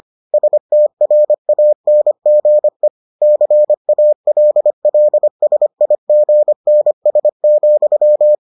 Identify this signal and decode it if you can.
Morse code, heard as a keyed tone, about 25 wpm, 600 Hz. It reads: STRANGE CALLSIGNS,